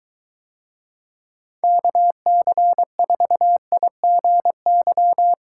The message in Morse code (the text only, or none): KC4IGY